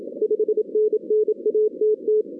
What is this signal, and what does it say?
Morse code, heard as a keyed tone, about 27 wpm, 430 Hz. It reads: 5NN ATT
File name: 2016 arrl dx cw / 2016-ARRL-DX-CW-K5TR- - 0-41